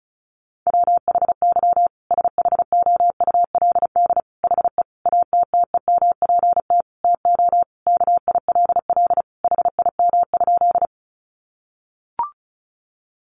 Morse code, heard as a keyed tone, about 34 wpm, 700 Hz. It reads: WHY SHOULD HE ATTEMPT TO KILL HIM?
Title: Sentences from Top 1400 Words 35wpm